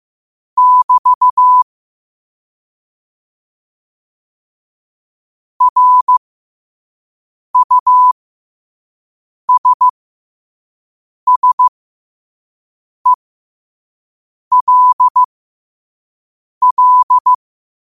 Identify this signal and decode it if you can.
Morse code, heard as a keyed tone, about 15 wpm, spaced as for 5 wpm, 1000 Hz. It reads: <BT> RUSSELL
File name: qotdfiles / 5wpm